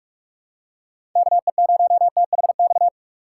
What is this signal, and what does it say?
Morse code, heard as a keyed tone, about 45 wpm, 700 Hz. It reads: KE0THX
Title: USA Call Signs - 2x Character Spacing - 45wpm